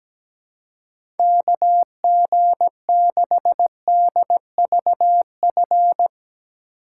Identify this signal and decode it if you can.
Morse code, heard as a keyed tone, about 17 wpm, 700 Hz. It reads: KG6DVF
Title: USA Call Signs - 6 Characters - 4x Character Spacing - 17wpm